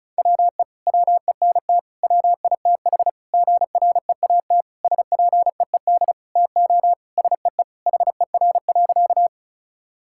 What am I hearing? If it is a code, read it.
Morse code, heard as a keyed tone, about 34 wpm, 700 Hz. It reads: WE WENT WITH GREAT SPEED TO SEE HER.
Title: Sentences from Top 1000 Words - Review Part 5 - 35wpm